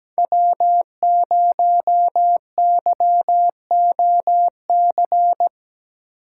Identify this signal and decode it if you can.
Morse code, heard as a keyed tone, about 17 wpm, 700 Hz. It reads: W0YOC
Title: USA Call Signs - 5 Characters - 3x Character Spacing - 17wpm